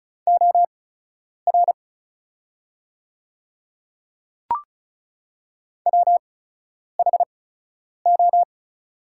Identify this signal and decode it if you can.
Morse code, heard as a keyed tone, about 35 wpm, spaced as for 10 wpm, 700 Hz. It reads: OR WHO
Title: 8x Character Spacing - Top 100 Words - 35wpm